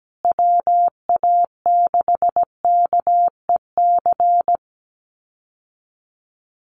Morse code, heard as a keyed tone, about 17 wpm, 700 Hz. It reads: WA6KEC
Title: USA Call Signs Repeated 2x 17wpm